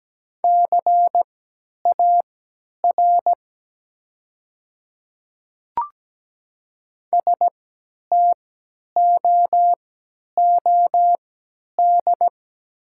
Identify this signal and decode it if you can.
Morse code, heard as a keyed tone, about 17 wpm, spaced as for 10 wpm, 700 Hz. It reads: CAR STOOD
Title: 3x Character Spacing - Top 500 Words - 17wpm